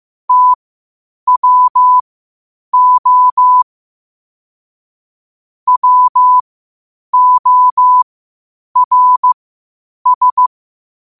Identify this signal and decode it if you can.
Morse code, heard as a keyed tone, about 15 wpm, spaced as for 8 wpm, 1000 Hz. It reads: TWO WORS
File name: qotdfiles / 8wpm